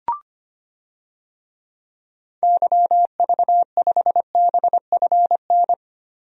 Morse code, heard as a keyed tone, about 25 wpm, 700 Hz. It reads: YV5BFN
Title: DXCC-100 Call Signs - 6 Characters - Repeated 2x - 25wpm